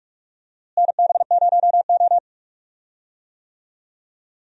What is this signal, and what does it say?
Morse code, heard as a keyed tone, about 46 wpm, 700 Hz. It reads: NB0O
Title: USA Call Signs - 4 Characters - Repeated 2x - 45wpm